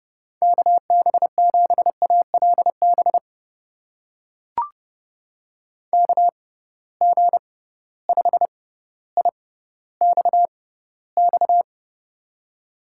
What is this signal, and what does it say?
Morse code, heard as a keyed tone, about 30 wpm, spaced as for 15 wpm, 700 Hz. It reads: KB7ALB KG5IXX
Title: USA Call Signs - 6x Character Spacing - 30wpm